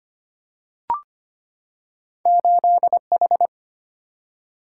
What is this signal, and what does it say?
Morse code, heard as a keyed tone, about 25 wpm, 700 Hz. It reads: 8H
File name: Two Letter-Number Combinations - 25wpm